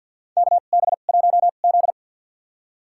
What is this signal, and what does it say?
Morse code, heard as a keyed tone, about 50 wpm, spaced as for 36 wpm, 700 Hz. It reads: KB1Z
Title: USA Call Signs - 4 Characters - 2x Character Spacing - 50wpm